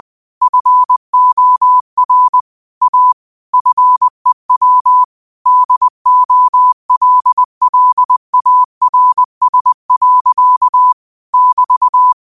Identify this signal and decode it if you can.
Morse code, heard as a keyed tone, about 20 wpm, 1000 Hz. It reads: FOR A FEW DOLLARS. <BT>